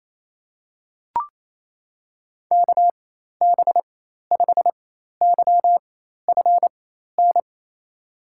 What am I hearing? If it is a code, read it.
Morse code, heard as a keyed tone, about 27 wpm, spaced as for 13 wpm, 700 Hz. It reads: KB5YFN
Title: USA Call Signs - 4x Character Spacing - 28wpm